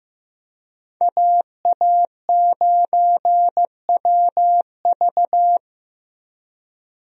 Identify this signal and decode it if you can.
Morse code, heard as a keyed tone, about 15 wpm, 700 Hz. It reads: AA9WV
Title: USA Call Signs - 5 Characters - Repeated 3x - 15wpm